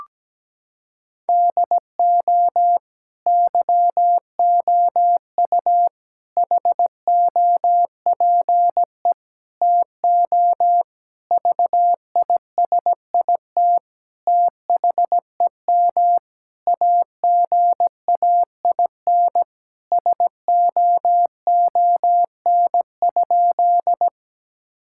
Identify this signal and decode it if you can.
Morse code, heard as a keyed tone, about 17 wpm, 700 Hz. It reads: DO YOU HOPE TO VISIT THEM AGAIN SOON?
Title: Sentences from Top 1000 Words - Review Part 6 - 17wpm